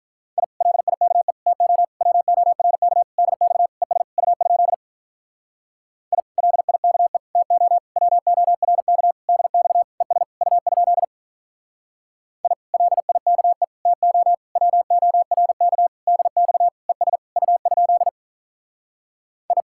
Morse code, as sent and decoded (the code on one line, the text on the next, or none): .. / .-.. .. -.- . / - --- / .-- --- .-. -.- / -.. -..- / . ... / ..- ..--.. / .. / .-.. .. -.- . / - --- / .-- --- .-. -.- / -.. -..- / . ... / ..- ..--.. / .. / .-.. .. -.- . / - --- / .-- --- .-. -.- / -.. -..- / . ... / ..- ..--.. / ..
I LIKE TO WORK DX ES U? I LIKE TO WORK DX ES U? I LIKE TO WORK DX ES U? I